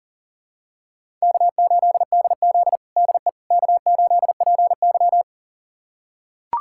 Morse code, heard as a keyed tone, about 37 wpm, 700 Hz. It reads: K8DZ DE K8PY
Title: USA Callsign DE Callsign - 4 Characters - 40wpm